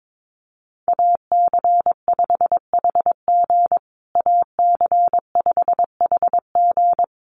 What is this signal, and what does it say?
Morse code, heard as a keyed tone, about 22 wpm, 700 Hz. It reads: AC5HG AC5HG
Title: USA Call Signs Repeated 2x 22wpm